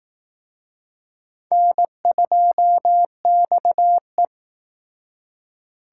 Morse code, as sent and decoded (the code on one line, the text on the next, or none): -. ..--- -..- .
N2XE